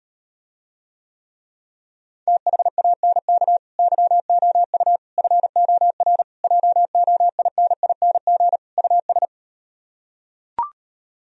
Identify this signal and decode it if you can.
Morse code, heard as a keyed tone, about 38 wpm, 700 Hz. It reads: THANK YOU FOR JOINING US